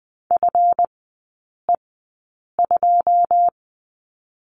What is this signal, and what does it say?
Morse code, heard as a keyed tone, about 20 wpm, spaced as for 8 wpm, 700 Hz. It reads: FE2
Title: 145 Lesson 24e - Letter y - Three Characters 20wpm